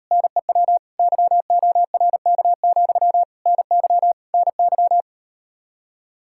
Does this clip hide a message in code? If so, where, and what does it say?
Yes: Morse code from 0.1 to 5.0 s, heard as a keyed tone, about 38 wpm, 700 Hz. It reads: NEW YORK, NY NY